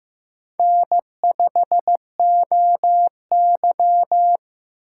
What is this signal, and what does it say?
Morse code, heard as a keyed tone, about 15 wpm, 700 Hz. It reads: N5OY
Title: USA Call Signs - 4 Characters - 4x Character Spacing - 15wpm